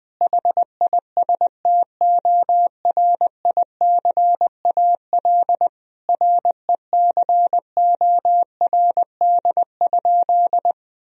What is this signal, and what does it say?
Morse code, heard as a keyed tone, about 19 wpm, 700 Hz. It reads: HISTORICAL RECORD?
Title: Sentences from Top 1000 Words - Review Part 4 - 20wpm